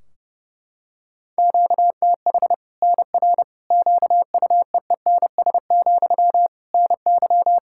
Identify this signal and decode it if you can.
Morse code, heard as a keyed tone, about 30 wpm, 700 Hz. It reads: QTH NR QUEENS, NY
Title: Practice QSO - Exchange 1 - Beginning - 30wpm